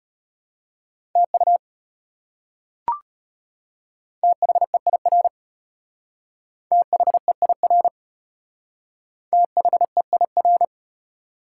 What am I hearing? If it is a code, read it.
Morse code, heard as a keyed tone, about 34 wpm, 700 Hz. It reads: TU THEIR THEIR THEIR